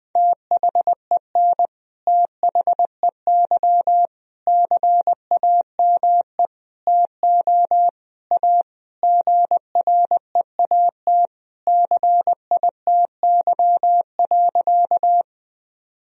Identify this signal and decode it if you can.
Morse code, heard as a keyed tone, about 20 wpm, 700 Hz. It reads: THEN THEY CAME TO A GREAT CITY.
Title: Sentences from Top 1000 Words - Review Part 1 - 20wpm